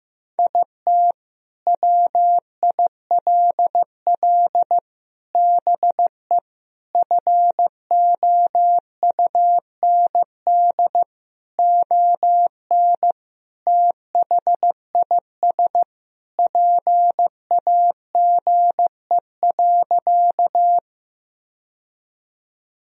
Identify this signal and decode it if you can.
Morse code, heard as a keyed tone, about 15 wpm, 700 Hz. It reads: IT WILL BE FOUND ON THIS PAGE.
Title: Sentences from Top 1000 Words - Review Part 1 - 15wpm